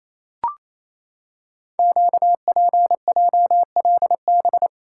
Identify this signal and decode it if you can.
Morse code, heard as a keyed tone, about 27 wpm, 700 Hz. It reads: QPJLB
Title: Code Groups - 5 Letters - 28wpm